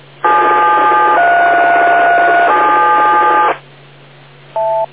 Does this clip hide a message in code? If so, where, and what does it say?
Yes: DTMF from 0.2 to 3.5 s, heard as telephone touch tones, tones of about 140 ms. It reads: 737